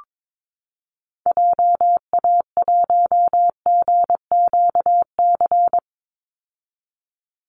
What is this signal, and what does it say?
Morse code, heard as a keyed tone, about 22 wpm, 700 Hz. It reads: JA1GQC
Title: Call Signs - 22wpm